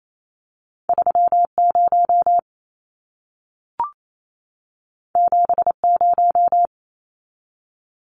Morse code, heard as a keyed tone, about 28 wpm, 700 Hz. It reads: 30 70
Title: Numbers - 2-Digit 28wpm